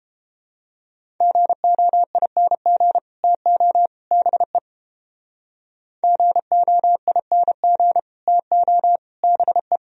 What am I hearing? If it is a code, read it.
Morse code, heard as a keyed tone, about 32 wpm, 700 Hz. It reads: GOING TO BE GOING TO BE